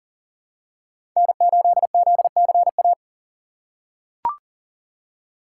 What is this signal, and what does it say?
Morse code, heard as a keyed tone, about 40 wpm, 700 Hz. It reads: N8ZCA
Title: USA Call Signs - 5 Characters - 3x Character Spacing - 40wpm